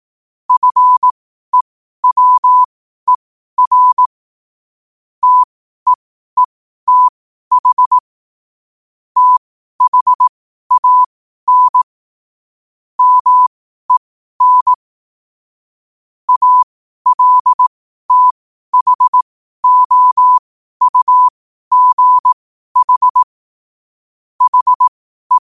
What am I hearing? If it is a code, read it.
Morse code, heard as a keyed tone, about 18 wpm, spaced as for 12 wpm, 1000 Hz. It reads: FEWER TEETH THAN MEN ALTHOUGH HE
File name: qotdfiles / 12wpm